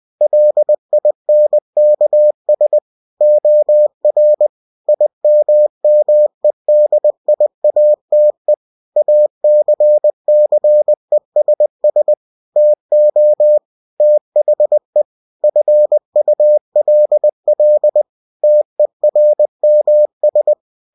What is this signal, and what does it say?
Morse code, heard as a keyed tone, about 20 wpm, 600 Hz. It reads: LINKS OR IMMEDIATE ACCESS TO THE FULL TERMS
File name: cw-books / winnie-the-pooh / winnie-20wpm-page-0116